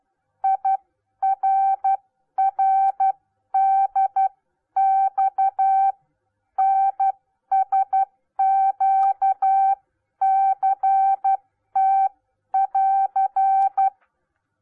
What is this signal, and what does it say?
Morse code, heard as a keyed tone, about 12 wpm, spaced as for 10 wpm, 780 Hz. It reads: IRRDX NSQCT<AR>